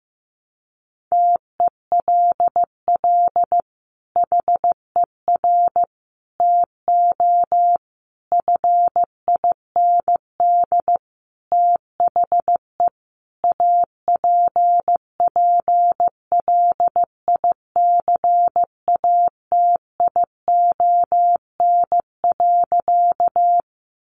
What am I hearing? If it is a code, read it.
Morse code, heard as a keyed tone, about 15 wpm, 700 Hz. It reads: TELL HER TO FIND THE APPLICATION.
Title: Sentences from Top 1200 Words 15wpm